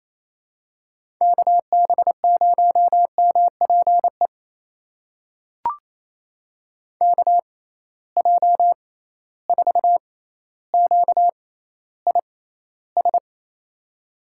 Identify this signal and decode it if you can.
Morse code, heard as a keyed tone, about 28 wpm, spaced as for 14 wpm, 700 Hz. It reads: KB0MPE KJ4QIS
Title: USA Call Signs - 6 Characters - 6x Character Spacing - 28wpm